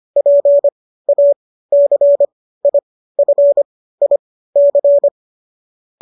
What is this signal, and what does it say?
Morse code, heard as a keyed tone, about 25 wpm, spaced as for 15 wpm, 570 Hz. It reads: PACIFIC
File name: qso213 15